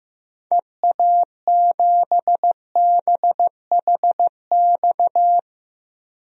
Morse code, heard as a keyed tone, about 15 wpm, 700 Hz. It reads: EA7BHX
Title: DXCC-100 Call Signs - 6 Characters - 15wpm